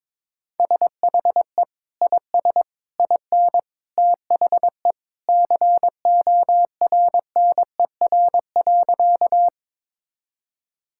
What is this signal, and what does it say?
Morse code, heard as a keyed tone, about 22 wpm, 700 Hz. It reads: SHE IS IN THE CORNER.
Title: Sentences from Top 1600 Words 22wpm